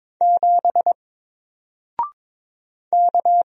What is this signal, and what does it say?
Morse code, heard as a keyed tone, about 22 wpm, 700 Hz. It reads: 7 K
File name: Single-Letter-Number-22wpm